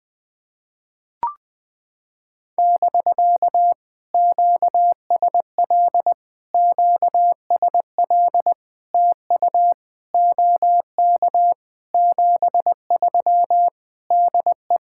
Morse code, heard as a keyed tone, about 20 wpm, 700 Hz. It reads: <BK> QSL QSL TU OK 73 DE
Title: POTA QSO Practice - 20wpm